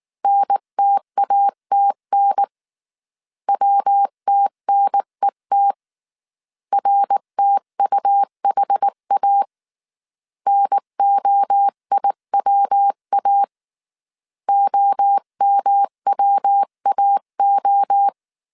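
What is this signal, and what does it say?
Morse code, heard as a keyed tone, about 19 wpm, 800 Hz. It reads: NTATN WTNET RTUHA NOIWA OMWAO
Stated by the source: Morse code dictation at 18 wpm from the IW3SGT QRS Morse code training course, five-character groups of 300 characters.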